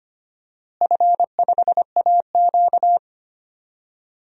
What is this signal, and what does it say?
Morse code, heard as a keyed tone, about 25 wpm, 700 Hz. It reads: F5AQ
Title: DXCC-100 Call Signs - 4 Characters - Repeated 2x - 25wpm